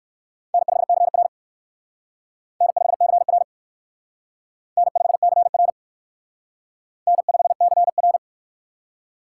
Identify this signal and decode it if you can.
Morse code, heard as a keyed tone, about 61 wpm, 700 Hz. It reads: N5CR N5CR N5CR N5CR